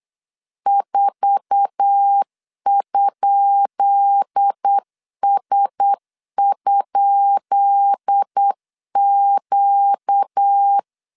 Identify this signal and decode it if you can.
Morse code, heard as a keyed tone, about 8 wpm, 800 Hz. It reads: 4?S?Q
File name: LEZIONI / 10 SIGNS / 10B INT 8wpm 3534